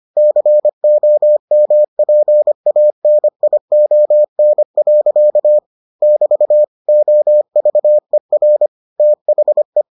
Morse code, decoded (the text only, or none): COMPANION. <BT> OVER THE